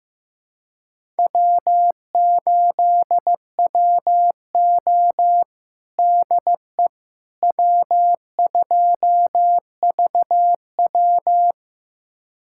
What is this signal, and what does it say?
Morse code, heard as a keyed tone, about 15 wpm, 700 Hz. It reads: W8WO DE W2VW
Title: USA Callsign DE Callsign - 4 Characters - 15wpm